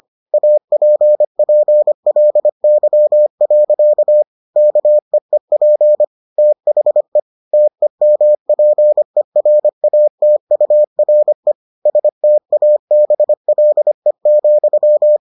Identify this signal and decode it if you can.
Morse code, heard as a keyed tone, about 25 wpm, 600 Hz. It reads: APPLY. KEEP THE TEMPERATURE STABLE,